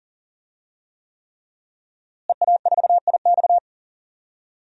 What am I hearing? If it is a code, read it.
Morse code, heard as a keyed tone, about 40 wpm, 700 Hz. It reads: EA4IX
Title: DXCC-100 Call Signs - 5 Characters - Repeated 3x - 40wpm